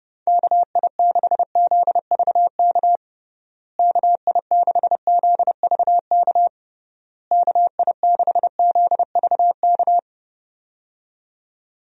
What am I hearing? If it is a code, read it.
Morse code, heard as a keyed tone, about 30 wpm, 700 Hz. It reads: KI6ZVK KI6ZVK KI6ZVK
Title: USA Call Signs - 6 Characters - Repeated 3x - 30wpm